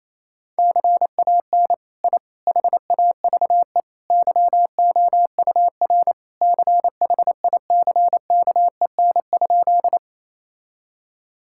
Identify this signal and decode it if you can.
Morse code, heard as a keyed tone, about 27 wpm, 700 Hz. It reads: CAN I HAVE YOUR CHICKEN?
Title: Sentences from Top 1000 Words - Review Part 5 - 28wpm